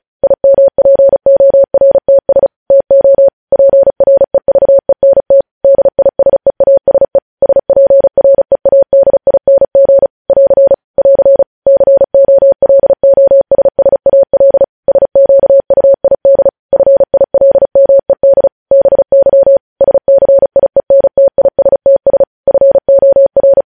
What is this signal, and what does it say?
Morse code, heard as a keyed tone, about 36 wpm, 550 Hz. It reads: IMPORTS TO PREVENT DISEASE SPREADING <AR> <AR> COLOSSAL SQUID FILMED BY SCIENTISTS FOR
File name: podcast / News headlines-1x-35x35